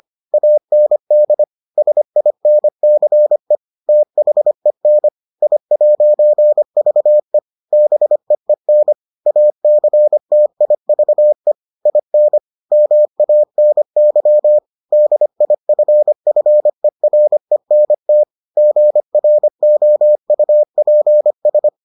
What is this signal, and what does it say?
Morse code, heard as a keyed tone, about 25 wpm, 600 Hz. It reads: AND SINCE THEN I'VE BEEN ACTIVE IN MANY DIFFERENT GROUPS